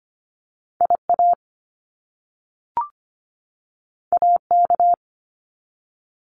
Morse code, heard as a keyed tone, about 25 wpm, 700 Hz. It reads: IA AK
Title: US States 25wpm